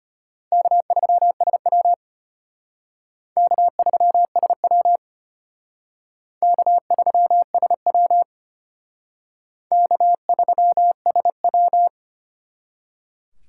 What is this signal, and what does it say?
Morse code, heard as a keyed tone, about 32 wpm, 700 Hz. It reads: K3SW K3SW K3SW K3SW